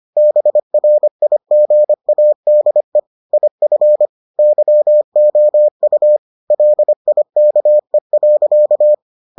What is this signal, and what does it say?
Morse code, heard as a keyed tone, about 25 wpm, 600 Hz. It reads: BRIGADE IF YOU LIKE.